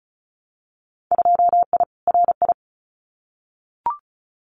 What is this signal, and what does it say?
Morse code, heard as a keyed tone, about 34 wpm, 700 Hz. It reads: 2I RI
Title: Winter Field Day Exchanges Encore 35wpm